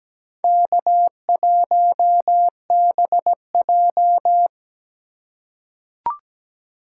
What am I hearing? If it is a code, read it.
Morse code, heard as a keyed tone, about 17 wpm, 700 Hz. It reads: K1BJ
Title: USA Call Signs - 4 Characters - 6x Character Spacing - 17wpm